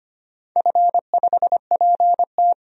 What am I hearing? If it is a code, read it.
Morse code, heard as a keyed tone, about 25 wpm, 690 Hz. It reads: F5PT
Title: DXCC-100 Call Signs - 4 Characters - 25wpm